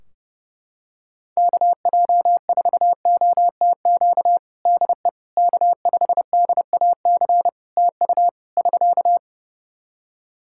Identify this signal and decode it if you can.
Morse code, heard as a keyed tone, about 30 wpm, 700 Hz. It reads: KJ4OTQ DE K5DAC TU <SK>
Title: Practice QSO - Exchange 4 - Ending - 30wpm